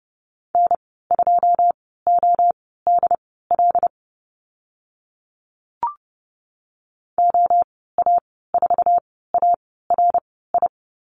Morse code, heard as a keyed tone, about 30 wpm, spaced as for 17 wpm, 700 Hz. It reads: N2ODL OA4ARI
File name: Lesson 14g - Number 2 - Call Signs 30wpm